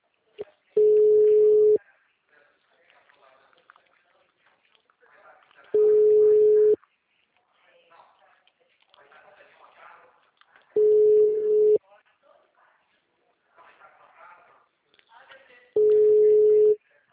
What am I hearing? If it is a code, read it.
Ringback tone (Europe, 425 Hz), with speech.